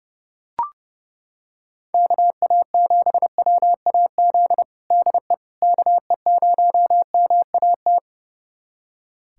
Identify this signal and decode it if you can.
Morse code, heard as a keyed tone, about 32 wpm, 700 Hz. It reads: KA7WAZ DE KE0MAT
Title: USA Callsign DE Callsign - 6 Characters - 30wpm